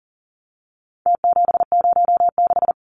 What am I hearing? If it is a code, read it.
Morse code, heard as a keyed tone, about 40 wpm, 700 Hz. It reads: T706
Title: Numbers - Contest Style - 4-Digit 40wpm